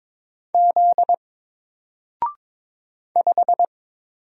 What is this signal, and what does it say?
Morse code, heard as a keyed tone, about 22 wpm, 700 Hz. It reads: Z 5
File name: Single-Letter-Number-22wpm